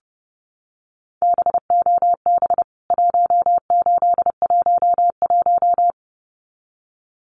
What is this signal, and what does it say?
Morse code, heard as a keyed tone, about 30 wpm, 700 Hz. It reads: BOB 1811
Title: CWT Exchange 30wpm